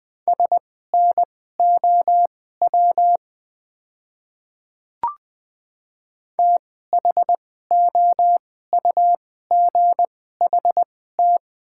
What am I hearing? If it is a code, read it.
Morse code, heard as a keyed tone, about 20 wpm, spaced as for 14 wpm, 690 Hz. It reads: SNOW THOUGHT
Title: 2x Character Spacing - Top 500 Words - 20wpm